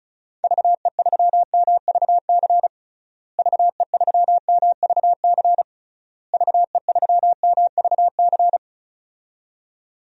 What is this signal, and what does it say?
Morse code, heard as a keyed tone, about 34 wpm, 700 Hz. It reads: VE3MVC VE3MVC VE3MVC
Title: DXCC-100 Call Signs - 6 Characters - Repeated 3x - 35wpm